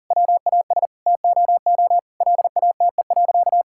Morse code, heard as a keyed tone, about 40 wpm, 700 Hz. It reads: WAS TOO LATE.
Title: Sentences from Top 1000 Words - Review Part 6 - 40wpm